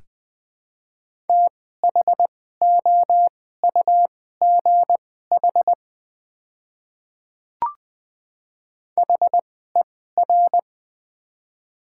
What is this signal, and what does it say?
Morse code, heard as a keyed tone, about 20 wpm, spaced as for 14 wpm, 700 Hz. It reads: THOUGH HER
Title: 2x Character Spacing - Top 500 Words - 20wpm